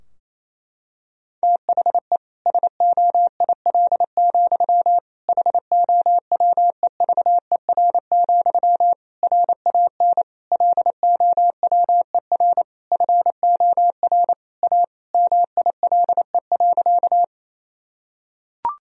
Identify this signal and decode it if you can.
Morse code, heard as a keyed tone, about 28 wpm, 700 Hz. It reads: THE SOIL, HOWEVER, RAN LOWER FOR A MILE.
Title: Sentences from Top 1000 Words - Review Part 4 - 28wpm